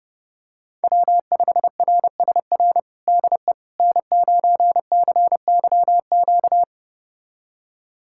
Morse code, heard as a keyed tone, about 32 wpm, 700 Hz. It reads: W5RSR DE N9CYQ
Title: USA Callsign DE Callsign - 5 Characters - 30wpm